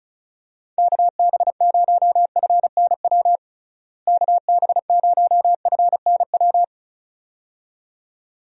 Morse code, heard as a keyed tone, about 36 wpm, 700 Hz. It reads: KB0FNW KB0FNW
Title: USA Call Signs - 6 Characters - Repeated 2x - 35wpm